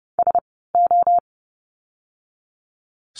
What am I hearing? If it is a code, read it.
Morse code, heard as a keyed tone, about 30 wpm, spaced as for 17 wpm, 700 Hz. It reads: SO
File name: course / 30:10wpm / 230 Lesson 37f - Letter x - Words 30wpm